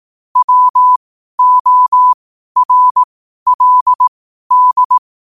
Morse code, heard as a keyed tone, about 18 wpm, spaced as for 12 wpm, 1000 Hz. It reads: WORLD